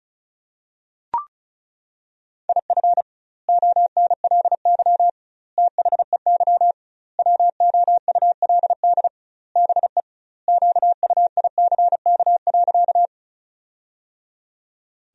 Morse code, heard as a keyed tone, about 34 wpm, 700 Hz. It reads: IF ONLY THEY WOULD BE QUICK.
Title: 2x Word-Spacing - Sentences from Top 500 Words - 35wpm